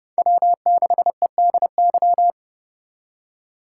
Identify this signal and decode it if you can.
Morse code, heard as a keyed tone, about 30 wpm, 700 Hz. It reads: W6EDY
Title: K1USN SST QSO Practice - 30wpm